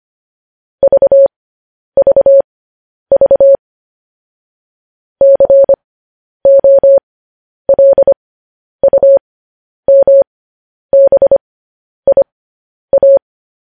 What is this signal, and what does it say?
Morse code, heard as a keyed tone, about 25 wpm, spaced as for 10 wpm, 550 Hz. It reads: VVV COLUMBIA
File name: podcast / News headlines-1x-25x10